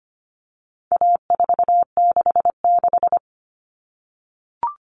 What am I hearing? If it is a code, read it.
Morse code, heard as a keyed tone, about 25 wpm, 700 Hz. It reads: A466
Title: Common Cut Numbers - 4-Digit 25wpm